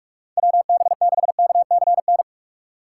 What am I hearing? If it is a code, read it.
Morse code, heard as a keyed tone, about 46 wpm, 700 Hz. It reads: WB6KCN
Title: USA Call Signs - 6 Characters - 2x Character Spacing - 45wpm